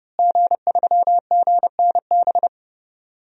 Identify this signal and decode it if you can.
Morse code, heard as a keyed tone, about 31 wpm, 700 Hz. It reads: G3GNB
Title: DXCC-100 Call Signs - 5 Characters - Repeated 3x - 30wpm